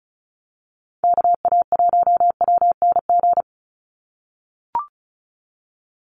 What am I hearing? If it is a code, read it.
Morse code, heard as a keyed tone, about 34 wpm, 700 Hz. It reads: KA1WNG
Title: USA Call Signs 35wpm